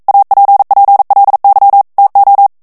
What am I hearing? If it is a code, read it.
Morse code, heard as a keyed tone, about 42 wpm, 800 Hz. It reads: APPLY TO